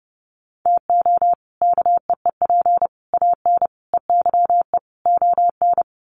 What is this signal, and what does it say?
Morse code, heard as a keyed tone, about 30 wpm, 700 Hz. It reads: TO KEEP AN EYE ON
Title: Sets of 5 Words Encore 30wpm